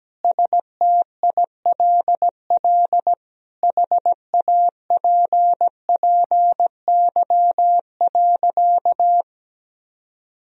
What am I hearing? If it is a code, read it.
Morse code, heard as a keyed tone, about 17 wpm, 700 Hz. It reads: STILL HAPPY.
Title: Sentences from Top 1000 Words - Review Part 3 - 17wpm